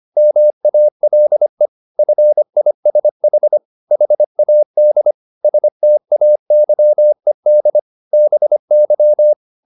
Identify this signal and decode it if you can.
Morse code, heard as a keyed tone, about 25 wpm, 600 Hz. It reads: MALE FISH HAD STAYED BY
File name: the old man and the sea part 1 25wpm